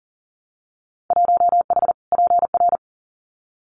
Morse code, heard as a keyed tone, about 39 wpm, 700 Hz. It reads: 1H PR
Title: Winter Field Day Exchanges Encore 40wpm